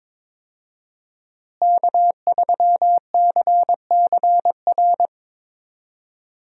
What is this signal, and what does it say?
Morse code, heard as a keyed tone, about 22 wpm, 700 Hz. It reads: K3CCR
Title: USA Call Signs - 5 Characters - Repeated 3x - 22wpm